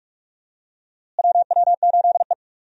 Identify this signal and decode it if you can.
Morse code, heard as a keyed tone, about 46 wpm, 700 Hz. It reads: WW8E
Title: USA Call Signs - 4 Characters - Repeated 3x - 45wpm